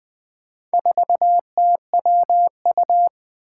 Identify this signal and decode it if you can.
Morse code, heard as a keyed tone, about 20 wpm, 700 Hz. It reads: 4TWU
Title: Four Challenging Letter-Number Combinations 20wpm